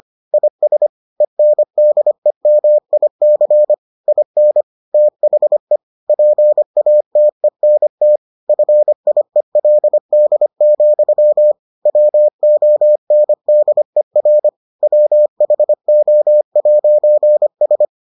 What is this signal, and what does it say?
Morse code, heard as a keyed tone, about 25 wpm, 600 Hz. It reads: IS ENDEMIC IN THE PATENT FIELD, WONDER WHO'S